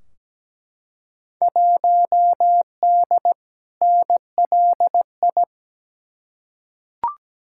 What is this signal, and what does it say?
Morse code, heard as a keyed tone, about 17 wpm, 700 Hz. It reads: 1D NLI
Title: Field Day Exchanges - 17wpm